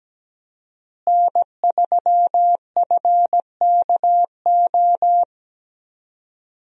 Morse code, heard as a keyed tone, about 17 wpm, 700 Hz. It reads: N3FKO
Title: USA Call Signs - 5 Characters - Repeated 2x - 17wpm